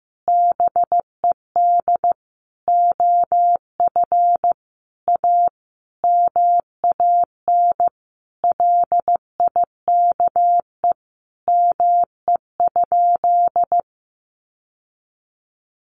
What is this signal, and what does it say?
Morse code, heard as a keyed tone, about 15 wpm, 700 Hz. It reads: BED OF A MAN LIKE ME?
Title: Sentences from Top 1200 Words 15wpm